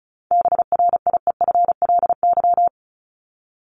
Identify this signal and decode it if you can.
Morse code, heard as a keyed tone, about 34 wpm, 700 Hz. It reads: BRIEFLY
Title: LY Suffix - Top 100 Words with Base Words 35wpm